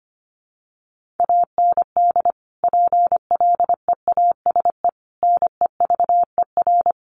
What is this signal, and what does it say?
Morse code, heard as a keyed tone, about 25 wpm, 700 Hz. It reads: AND PLEASE NEVER
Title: Sentences from Top 1500 Words 25wpm